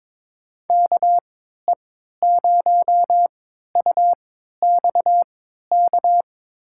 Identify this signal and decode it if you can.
Morse code, heard as a keyed tone, about 22 wpm, spaced as for 12 wpm, 700 Hz. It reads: KE0UXK
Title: USA Call Signs - 6 Characters - 3x Character Spacing - 22wpm